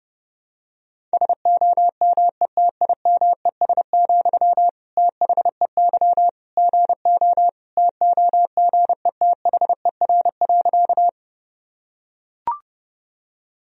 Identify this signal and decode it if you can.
Morse code, heard as a keyed tone, about 30 wpm, 700 Hz. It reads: SOMETIMES, THEY GO TOGETHER.
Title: Sentences from Top 1000 Words - Review Part 3 - 30wpm